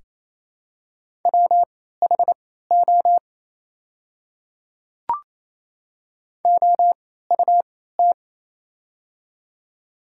Morse code, heard as a keyed tone, about 28 wpm, spaced as for 16 wpm, 700 Hz. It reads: WHO OUT
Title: 3x Character Spacing - Top 300 Words - 28wpm